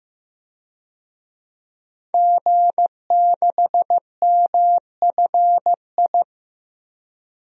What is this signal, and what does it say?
Morse code, heard as a keyed tone, about 15 wpm, 700 Hz. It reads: G6MFI
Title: DXCC-100 Call Signs - 5 Characters - 15wpm